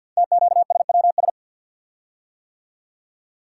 Morse code, heard as a keyed tone, about 50 wpm, 690 Hz. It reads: TQSWS
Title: Code Groups - 5 Letters - 50wpm